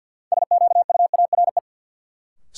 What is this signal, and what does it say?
Morse code, heard as a keyed tone, about 52 wpm, 700 Hz. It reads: SQUARE
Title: Sentences from Top 1200 Words - New Word Drill Alt - 50wpm